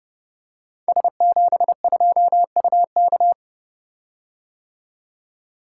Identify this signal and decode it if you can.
Morse code, heard as a keyed tone, about 30 wpm, 700 Hz. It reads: S72UK
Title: Code Groups - 5 Letter:Numbers - 30wpm